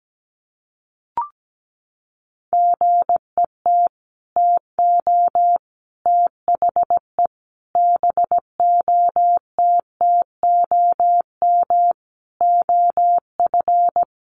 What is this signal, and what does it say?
Morse code, heard as a keyed tone, about 17 wpm, 700 Hz. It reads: GET TO THE BOTTOM OF
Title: Sets of 5 Words Encore 17wpm